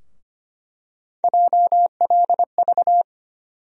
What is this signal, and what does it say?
Morse code, heard as a keyed tone, about 25 wpm, 700 Hz. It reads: JLV
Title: USA Call Sign Postfixes - 3 Letters 25wpm